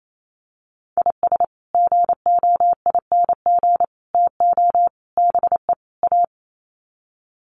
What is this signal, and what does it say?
Morse code, heard as a keyed tone, about 28 wpm, 700 Hz. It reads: IS GOING TO BE A
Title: Sets of 5 Words 28wpm